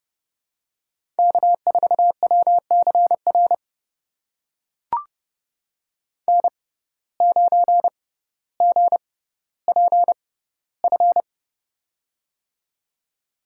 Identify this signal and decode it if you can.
Morse code, heard as a keyed tone, about 31 wpm, spaced as for 15 wpm, 700 Hz. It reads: K4WCR N9GPF
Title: USA Call Signs - 5 Characters - 6x Character Spacing - 30wpm